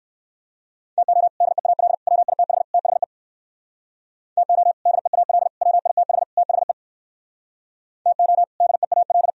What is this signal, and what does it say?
Morse code, heard as a keyed tone, about 57 wpm, 700 Hz. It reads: TO DEAL WITH THE TO DEAL WITH THE TO DEAL